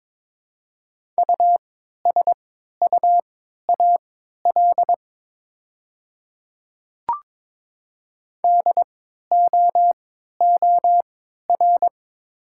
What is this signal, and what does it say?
Morse code, heard as a keyed tone, about 22 wpm, spaced as for 12 wpm, 690 Hz. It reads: USUAL DOOR
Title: 3x Character Spacing - Top 500 Words - 22wpm